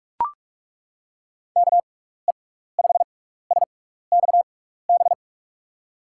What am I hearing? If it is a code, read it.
Morse code, heard as a keyed tone, about 41 wpm, 700 Hz. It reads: K E 5 S X B
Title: USA Call Signs - 6 Characters - 6x Character Spacing - 45wpm